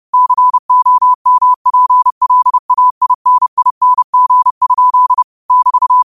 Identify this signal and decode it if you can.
Morse code, heard as a keyed tone, about 30 wpm, 1000 Hz. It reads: COMPLAINING? <BT>
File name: qotdfiles / 30wpm